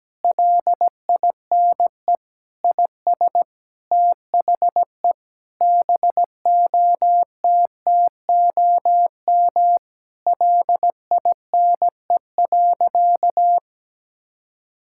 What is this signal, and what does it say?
Morse code, heard as a keyed tone, about 17 wpm, 700 Hz. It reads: LINE IS THE BOTTOM LINE.
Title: Sentences from Top 1000 Words - Review Part 6 - 17wpm